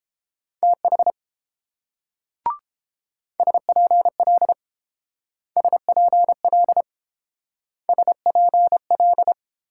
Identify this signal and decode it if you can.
Morse code, heard as a keyed tone, about 30 wpm, 700 Hz. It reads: TH SPL SPL SPL